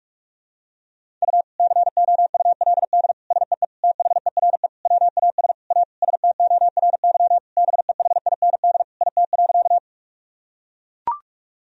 Morse code, heard as a keyed tone, about 42 wpm, 700 Hz. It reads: U COULD SEE THERE WAS A STORY BEHIND IT.
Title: Sentences from Top 500 Words and Abbreviations 45wpm